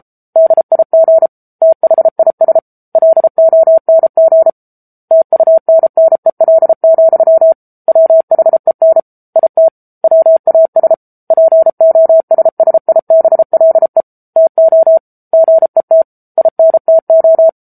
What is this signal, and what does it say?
Morse code, heard as a keyed tone, about 33 wpm, 650 Hz. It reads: DIG THIS LONG TUNNEL, WHEN IT WAS POSSIBLE TO GET INTO
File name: Book Two / 07-Chapter 7 - 33 WPM